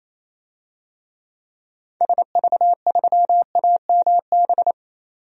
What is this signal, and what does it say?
Morse code, heard as a keyed tone, about 28 wpm, 700 Hz. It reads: SV3AMB
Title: DXCC-100 Call Signs - 6 Characters - 28wpm